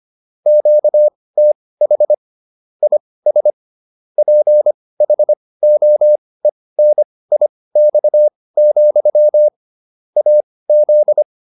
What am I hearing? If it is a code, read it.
Morse code, heard as a keyed tone, about 25 wpm, spaced as for 18 wpm, 600 Hz. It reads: QTH IS PHOENIX, AZ